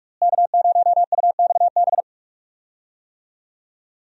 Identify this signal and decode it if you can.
Morse code, heard as a keyed tone, about 48 wpm, 700 Hz. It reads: K0UXB
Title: USA Call Signs - 5 Characters - Repeated 2x - 45wpm